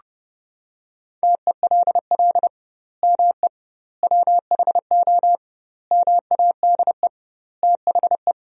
Morse code, heard as a keyed tone, about 32 wpm, 700 Hz. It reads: TELL ME WHO MADE THE
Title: 2x Word-Spacing - Sentences from Top 200 Words - 30wpm